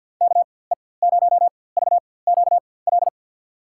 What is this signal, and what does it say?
Morse code, heard as a keyed tone, about 51 wpm, spaced as for 23 wpm, 700 Hz. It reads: KE0VQL
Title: USA Call Signs - 4x Character Spacing - 50wpm